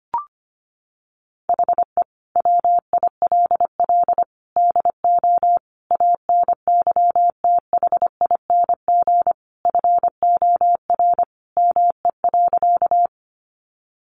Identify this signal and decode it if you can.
Morse code, heard as a keyed tone, about 25 wpm, 700 Hz. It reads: HE WILL DO ANYTHING FOR ME.
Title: Sentences from Top 1000 Words 25wpm